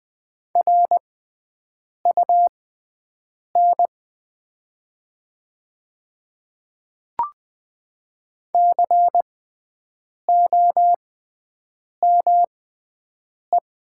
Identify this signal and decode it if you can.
Morse code, heard as a keyed tone, about 20 wpm, spaced as for 7 wpm, 700 Hz. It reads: RUN COME
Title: 6x Character Spacing - Top 300 Words - 20wpm